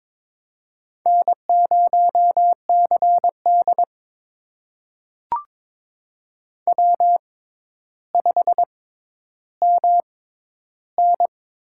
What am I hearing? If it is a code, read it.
Morse code, heard as a keyed tone, about 22 wpm, spaced as for 11 wpm, 700 Hz. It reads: N0CD W5MN
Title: USA Call Signs - 6x Character Spacing - 22wpm